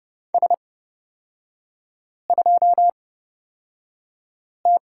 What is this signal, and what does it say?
Morse code, heard as a keyed tone, about 30 wpm, spaced as for 5 wpm, 700 Hz. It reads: S2T
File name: Lesson 37e - Letter x - Three Characters 30wpm